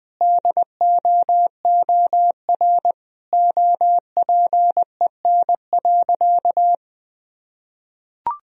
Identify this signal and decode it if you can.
Morse code, heard as a keyed tone, about 20 wpm, 700 Hz. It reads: DOOR OPEN.
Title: Sentences from Top 1000 Words - Review Part 1 - 20wpm